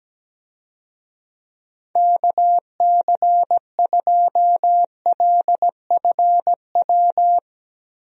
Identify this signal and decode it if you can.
Morse code, heard as a keyed tone, about 17 wpm, 700 Hz. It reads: KC2LFW
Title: USA Call Signs - 6 Characters - Repeated 2x - 17wpm